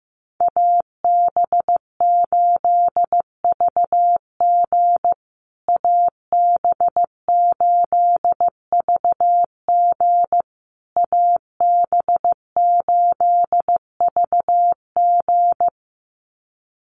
Morse code, heard as a keyed tone, about 15 wpm, 700 Hz. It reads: AB8VG AB8VG AB8VG
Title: USA Call Signs Repeated 3x 15wpm